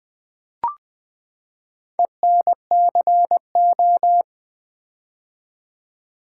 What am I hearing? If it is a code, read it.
Morse code, heard as a keyed tone, about 20 wpm, 700 Hz. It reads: ENCO
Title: Build A Word - 4 Letters At A Time - 11-Letter Words - 20wpm